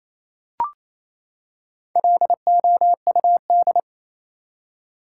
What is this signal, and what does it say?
Morse code, heard as a keyed tone, about 28 wpm, 700 Hz. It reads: LOUD
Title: Four Letter Words - 28wpm